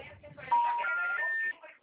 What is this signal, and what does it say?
Special information tone, with speech.